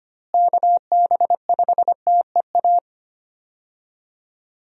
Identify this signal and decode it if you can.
Morse code, heard as a keyed tone, about 25 wpm, 700 Hz. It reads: KB5TEA
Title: USA Call Signs - 6 Characters - Repeated 3x - 25wpm